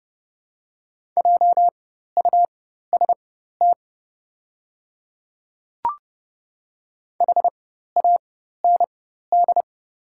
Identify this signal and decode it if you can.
Morse code, heard as a keyed tone, about 30 wpm, spaced as for 14 wpm, 700 Hz. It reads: JUST HAND
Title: 4x Character Spacing - Top 200 Words - 30wpm